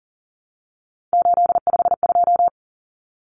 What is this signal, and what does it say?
Morse code, heard as a keyed tone, about 40 wpm, 700 Hz. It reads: 852
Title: Common Cut Numbers - 3-Digit 40wpm